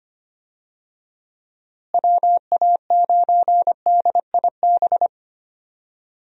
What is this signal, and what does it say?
Morse code, heard as a keyed tone, about 25 wpm, 700 Hz. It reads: WA9DIB
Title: USA Call Signs - 6 Characters - Repeated 2x - 25wpm